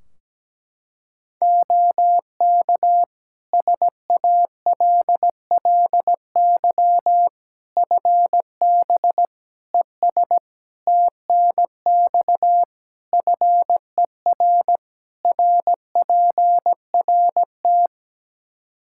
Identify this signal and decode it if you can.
Morse code, heard as a keyed tone, about 17 wpm, 700 Hz. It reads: OK SALLY FB ES TNX FER RPRT
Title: Practice QSO - Exchange 2 - Info - 17wpm